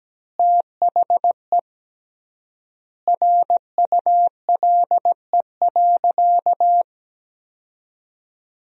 Morse code, heard as a keyed tone, about 17 wpm, 700 Hz. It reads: THE RULE.
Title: 3x Word-Spacing - Sentences from Top 400 Words - 17wpm